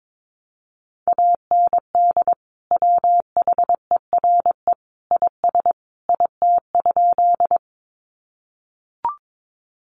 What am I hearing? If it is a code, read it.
Morse code, heard as a keyed tone, about 22 wpm, 700 Hz. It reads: AND WHERE IS IT?
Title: Sentences from Top 200 Words 22wpm